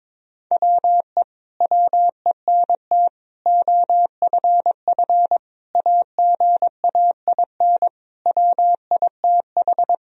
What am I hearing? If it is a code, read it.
Morse code, heard as a keyed tone, about 22 wpm, 700 Hz. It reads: WE WENT OFF AGAIN WITH
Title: Sentences from Top 1000 Words - Review Part 3 - 22wpm